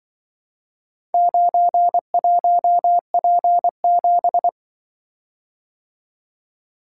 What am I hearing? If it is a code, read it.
Morse code, heard as a keyed tone, about 24 wpm, 700 Hz. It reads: 91P7
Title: Speed-Racing - Four Letter-Number Combinations - 20wpm